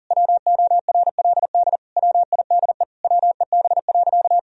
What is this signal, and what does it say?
Morse code, heard as a keyed tone, about 40 wpm, 700 Hz. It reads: WORLD WIDE WEB.
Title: Sentences from Top 1000 Words - Review Part 4 - 40wpm